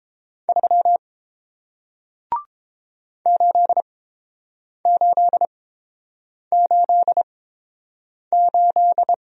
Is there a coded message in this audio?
Yes: Morse code from 0.5 to 9.1 s, heard as a keyed tone, about 30 wpm, spaced as for 7 wpm, 700 Hz. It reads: 3 8888